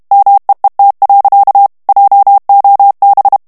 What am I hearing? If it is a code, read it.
Morse code, heard as a keyed tone, about 32 wpm, 800 Hz. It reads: MEET. JOB
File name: morse mp3 / ANGLAIS texte clair 1 vitesse 30 mots minute